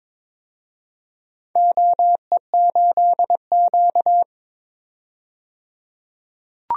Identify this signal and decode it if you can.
Morse code, heard as a keyed tone, about 22 wpm, 700 Hz. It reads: OE8Q
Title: DXCC-100 Call Signs - 4 Characters - 22wpm